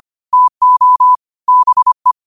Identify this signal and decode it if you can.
Morse code, heard as a keyed tone, about 25 wpm, 1000 Hz. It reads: TO BE